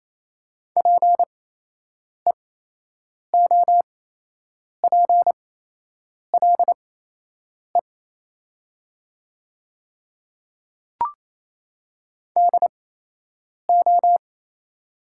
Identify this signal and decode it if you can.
Morse code, heard as a keyed tone, about 27 wpm, spaced as for 8 wpm, 700 Hz. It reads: PEOPLE DO
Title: 8x Character Spacing - Top 200 Words - 28wpm